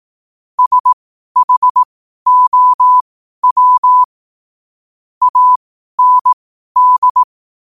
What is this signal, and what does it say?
Morse code, heard as a keyed tone, about 18 wpm, spaced as for 12 wpm, 1000 Hz. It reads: SHOW AND
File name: qotdfiles / 12wpm